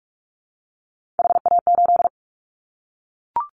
Morse code, heard as a keyed tone, about 44 wpm, 700 Hz. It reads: HA8
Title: Callsign Prefixes for Top 25 DXCC - Repeated 3x - 45wpm